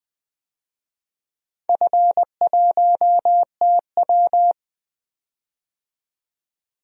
Morse code, heard as a keyed tone, about 20 wpm, 700 Hz. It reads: F1TW
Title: DXCC-100 Call Signs - 4 Characters - 20wpm